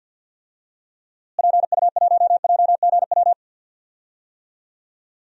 Morse code, heard as a keyed tone, about 47 wpm, 700 Hz. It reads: PU1JGW
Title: DXCC-100 Call Signs - 6 Characters - Repeated 2x - 50wpm